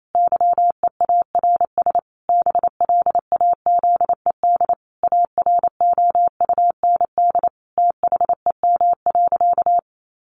Morse code, heard as a keyed tone, about 28 wpm, 700 Hz. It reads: YEARS BLAZED AROUND THEM.
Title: A Sound Of Thunder 28wpm